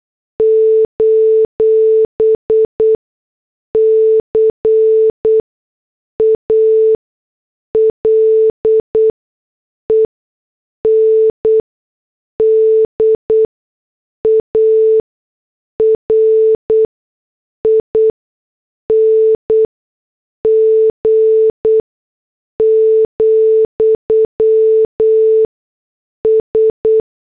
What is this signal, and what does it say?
Morse code, heard as a keyed tone, about 8 wpm, 440 Hz. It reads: : C A L E N D A R I N G , S
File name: morse / headlines